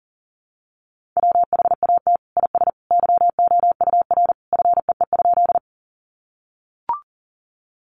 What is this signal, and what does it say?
Morse code, heard as a keyed tone, about 37 wpm, 700 Hz. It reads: WHAT IS YOUR FEE?
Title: Sentences from Top 1500 Words 40wpm